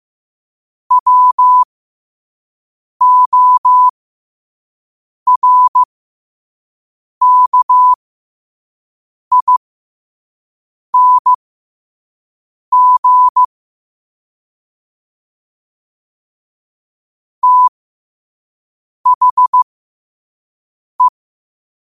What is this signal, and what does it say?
Morse code, heard as a keyed tone, about 15 wpm, spaced as for 5 wpm, 1000 Hz. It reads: WORKING THE
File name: qotdfiles / 5wpm